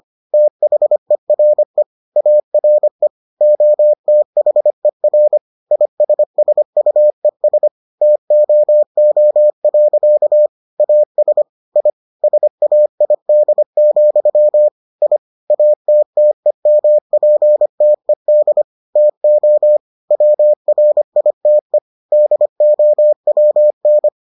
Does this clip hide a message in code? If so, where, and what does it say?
Yes: Morse code from 0.3 to 24.1 s, heard as a keyed tone, about 25 wpm, 600 Hz. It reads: THERE ARE OTHER ISSUES TOO. AS I SAID, I ATTEMPTED TO WRITE DOWN